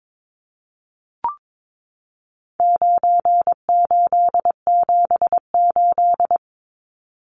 Morse code, heard as a keyed tone, about 22 wpm, 700 Hz. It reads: 9878
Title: Numbers - 4-Digit 22wpm